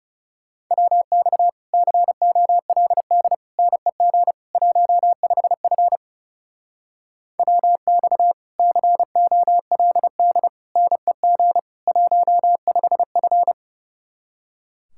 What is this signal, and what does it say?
Morse code, heard as a keyed tone, about 37 wpm, 700 Hz. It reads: WX COLD NEG 15F WX COLD NEG 15F
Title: Speed-Racing Common QSO Phrases - 30wpm